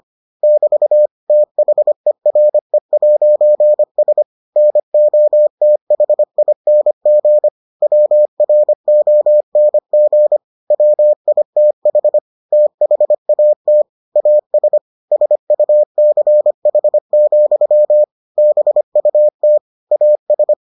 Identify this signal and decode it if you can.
Morse code, heard as a keyed tone, about 25 wpm, 600 Hz. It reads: <BT> THERE'S NOTHING WRONG WITH THAT AS SUCH, BUT AS